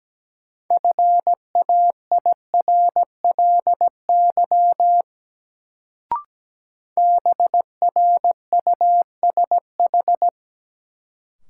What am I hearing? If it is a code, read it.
Morse code, heard as a keyed tone, about 17 wpm, 700 Hz. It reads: FAIRLY BRUSH
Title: Sentences from Top 1100 Words - New Word Drill Alt - 17wpm